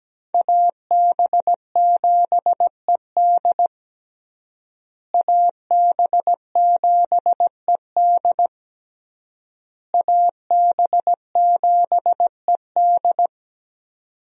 Morse code, heard as a keyed tone, about 17 wpm, 700 Hz. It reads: AB7ED AB7ED AB7ED
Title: USA Call Signs - 5 Characters - Repeated 3x - 17wpm